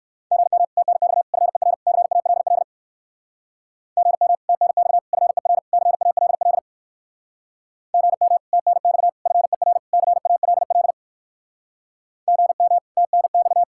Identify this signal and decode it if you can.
Morse code, heard as a keyed tone, about 57 wpm, 700 Hz. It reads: GM TNX FER CALL GM TNX FER CALL GM TNX FER CALL GM TNX